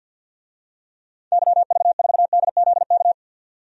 Morse code, heard as a keyed tone, about 50 wpm, 700 Hz. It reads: YV4DZK
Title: DXCC-100 Call Signs - 6 Characters - 50wpm